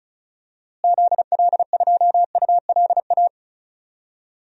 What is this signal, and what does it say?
Morse code, heard as a keyed tone, about 34 wpm, 700 Hz. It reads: ZL2ULA
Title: DXCC-100 Call Signs - 6 Characters - Repeated 2x - 35wpm